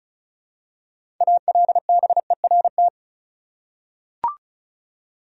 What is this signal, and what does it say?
Morse code, heard as a keyed tone, about 34 wpm, 700 Hz. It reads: ALBERT
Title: USA Common Names - 35wpm